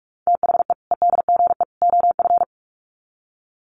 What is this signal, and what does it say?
Morse code, heard as a keyed tone, about 44 wpm, 700 Hz. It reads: THE EDGE OF
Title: Sets of 3 Words 45wpm